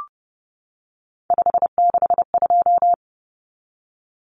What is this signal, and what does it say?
Morse code, heard as a keyed tone, about 30 wpm, 700 Hz. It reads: 562
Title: Numbers - 3-Digit 30wpm